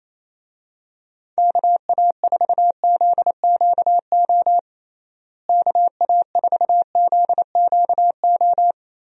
Morse code, heard as a keyed tone, about 28 wpm, 700 Hz. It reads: KA4ZQO KA4ZQO
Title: USA Call Signs - 6 Characters - Repeated 3x - 28wpm